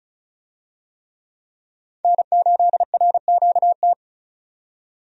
Morse code, heard as a keyed tone, about 34 wpm, 700 Hz. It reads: N8RQT
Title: USA Call Signs - 5 Characters - 35wpm